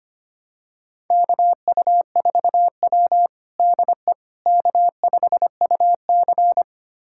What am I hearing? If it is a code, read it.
Morse code, heard as a keyed tone, about 25 wpm, 700 Hz. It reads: KU4W DE K5UC
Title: USA Callsign DE Callsign - 4 Characters - 25wpm